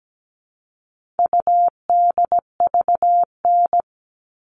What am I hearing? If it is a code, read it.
Morse code, heard as a keyed tone, about 17 wpm, 700 Hz. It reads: UDVN
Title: All Cut Numbers - 4-Digit 17wpm